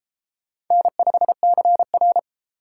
Morse code, heard as a keyed tone, about 32 wpm, 700 Hz. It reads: N5CR